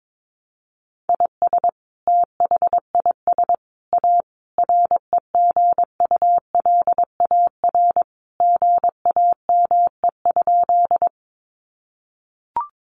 Morse code, heard as a keyed tone, about 22 wpm, 700 Hz. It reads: IS THIS A REGULAR GAME?
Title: Sentences from Top 900 Words 22wpm